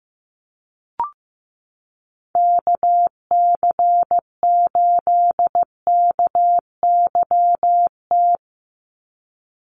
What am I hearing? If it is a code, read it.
Morse code, heard as a keyed tone, about 15 wpm, 700 Hz. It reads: KC8KYT
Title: USA Call Signs 15wpm